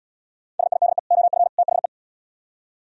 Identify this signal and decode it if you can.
Morse code, heard as a keyed tone, about 77 wpm, 700 Hz. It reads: SIDE OF THE